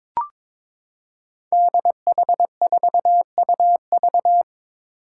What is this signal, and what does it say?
Morse code, heard as a keyed tone, about 22 wpm, 700 Hz. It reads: DH4UV
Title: Five Challenging Letter-Number Combinations 22wpm